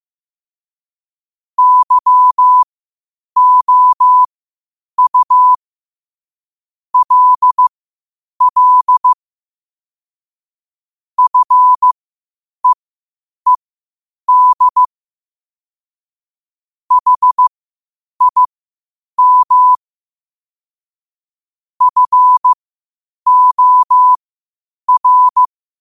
Morse code, heard as a keyed tone, about 15 wpm, spaced as for 8 wpm, 1000 Hz. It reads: YOU LL FEED HIM FOR
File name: qotdfiles / 8wpm